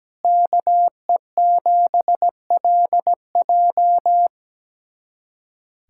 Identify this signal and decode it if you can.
Morse code, heard as a keyed tone, about 17 wpm, 700 Hz. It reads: KE7LJ
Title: USA Callsign DE Callsign - 5 Characters - 17wpm